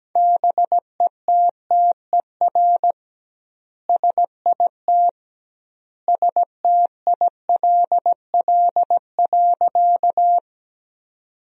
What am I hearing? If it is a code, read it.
Morse code, heard as a keyed tone, about 17 wpm, 700 Hz. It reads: BETTER SIT STILL.
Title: 2x Word-Spacing - Sentences from Top 500 Words - 17wpm